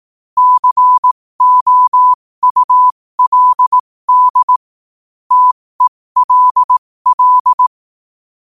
Morse code, heard as a keyed tone, about 18 wpm, spaced as for 15 wpm, 1000 Hz. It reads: COULD TELL
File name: qotdfiles / 15wpm